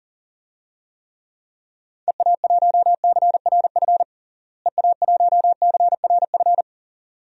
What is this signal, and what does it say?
Morse code, heard as a keyed tone, about 40 wpm, 700 Hz. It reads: EA1CRF EA1CRF
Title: DXCC-100 Call Signs - 6 Characters - Repeated 2x - 40wpm